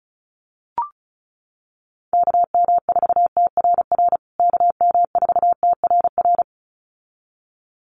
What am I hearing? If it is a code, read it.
Morse code, heard as a keyed tone, about 34 wpm, 700 Hz. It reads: KM4TRR KM4TRR
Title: USA Call Signs Repeated 2x 35wpm